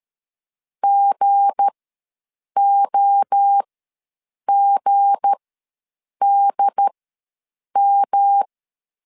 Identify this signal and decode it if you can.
Morse code, heard as a keyed tone, about 13 wpm, spaced as for 7 wpm, 800 Hz. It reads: GOGDM
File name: LEZIONI / 7 GF9Z / 7 G ws 12wpm 3140